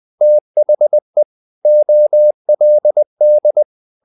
Morse code, heard as a keyed tone, about 20 wpm, 600 Hz. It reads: THE OLD
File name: the old man and the sea / the old man and the sea part 1 20wpm